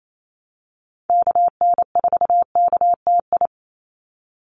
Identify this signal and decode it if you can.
Morse code, heard as a keyed tone, about 28 wpm, 700 Hz. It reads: KN4KTI
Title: USA Call Signs 28wpm